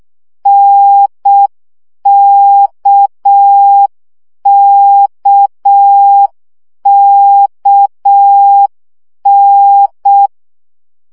Morse code, heard as a keyed tone, about 6 wpm, 800 Hz. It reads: NKKKN
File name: CW gyakorlas n d k 5 baud